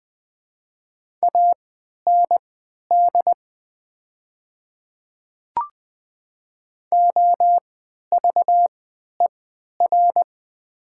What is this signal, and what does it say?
Morse code, heard as a keyed tone, about 20 wpm, spaced as for 11 wpm, 700 Hz. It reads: AND OVER
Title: 3x Character Spacing - Top 300 Words - 20wpm